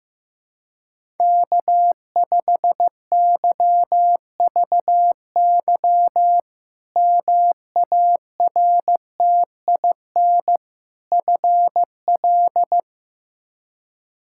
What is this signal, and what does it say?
Morse code, heard as a keyed tone, about 15 wpm, 700 Hz. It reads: K5YVY MARTIN FL
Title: K1USN SST QSO Practice - 15wpm